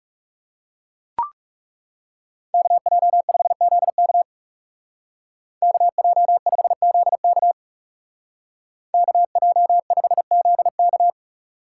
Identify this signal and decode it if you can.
Morse code, heard as a keyed tone, about 36 wpm, 700 Hz. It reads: KJ5ZK KJ5ZK KJ5ZK
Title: Speed Racing - USA 2x2 Call Signs 30wpm